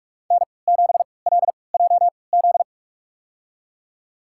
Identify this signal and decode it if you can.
Morse code, heard as a keyed tone, about 46 wpm, spaced as for 25 wpm, 700 Hz. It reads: N7LJZ
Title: USA Call Signs - 5 Characters - 3x Character Spacing - 45wpm